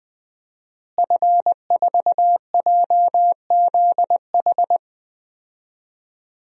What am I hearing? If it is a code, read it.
Morse code, heard as a keyed tone, about 20 wpm, 700 Hz. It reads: F4JZH
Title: DXCC-100 Call Signs - 5 Characters - Repeated 2x - 20wpm